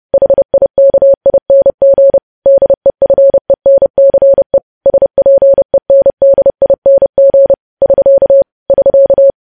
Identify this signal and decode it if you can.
Morse code, heard as a keyed tone, about 30 wpm, 550 Hz. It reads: HIKING DEFENCE SPENDING <SK> <SK>
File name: podcast / News headlines-1x-30x30